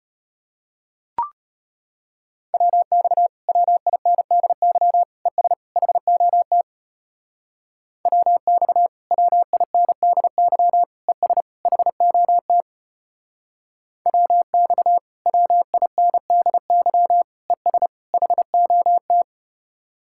Morse code, heard as a keyed tone, about 33 wpm, 700 Hz. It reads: WX WINDY ES HOT WX WINDY ES HOT WX WINDY ES HOT